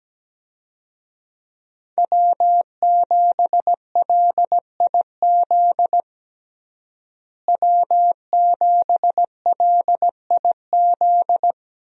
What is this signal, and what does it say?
Morse code, heard as a keyed tone, about 17 wpm, 700 Hz. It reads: W7LIZ W7LIZ
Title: USA Call Signs - 5 Characters - Repeated 2x - 17wpm